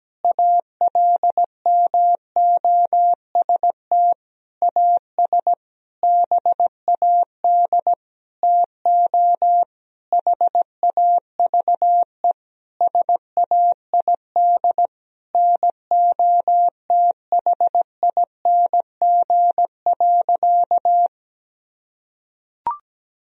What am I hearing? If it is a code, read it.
Morse code, heard as a keyed tone, about 17 wpm, 700 Hz. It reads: ALMOST AS BAD TO HAVE SAID NOTHING.
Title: Sentences from Top 1000 Words - Review Part 4 - 17wpm